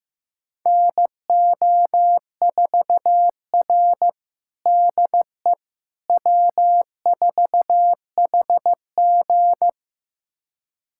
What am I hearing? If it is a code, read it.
Morse code, heard as a keyed tone, about 15 wpm, 700 Hz. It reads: NO4R DE W4HG
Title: USA Callsign DE Callsign - 4 Characters - 15wpm